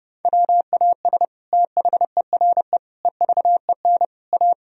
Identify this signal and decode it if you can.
Morse code, heard as a keyed tone, about 30 wpm, 700 Hz. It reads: WAS THERE EVEN A
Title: Sentences from Top 1000 Words - Review Part 4 - 30wpm